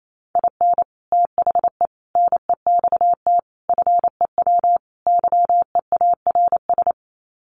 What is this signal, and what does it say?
Morse code, heard as a keyed tone, about 28 wpm, 700 Hz. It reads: IN THE NEXT FEW YEARS
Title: Sets of 5 Words Encore 28wpm